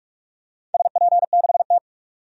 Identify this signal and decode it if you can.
Morse code, heard as a keyed tone, about 44 wpm, 700 Hz. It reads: SP6T
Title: DXCC-100 Call Signs - 4 Characters - 45wpm